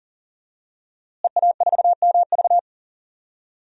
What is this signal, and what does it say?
Morse code, heard as a keyed tone, about 40 wpm, 700 Hz. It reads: EA4MV
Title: DXCC-100 Call Signs - 5 Characters - Repeated 3x - 40wpm